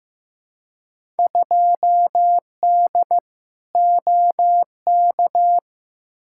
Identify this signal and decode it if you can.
Morse code, heard as a keyed tone, about 15 wpm, 700 Hz. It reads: 2D OK
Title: Field Day Exchanges Encore - 15wpm